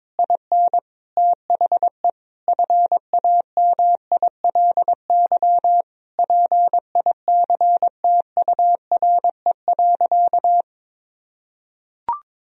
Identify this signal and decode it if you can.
Morse code, heard as a keyed tone, about 21 wpm, 700 Hz. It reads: IN THE FAMILY PICTURE.
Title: Sentences from Top 1000 Words - Review Part 3 - 22wpm